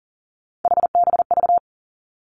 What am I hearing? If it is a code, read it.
Morse code, heard as a keyed tone, about 38 wpm, 700 Hz. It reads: HBV
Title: Three Challenging Letter-Number Combinations 40wpm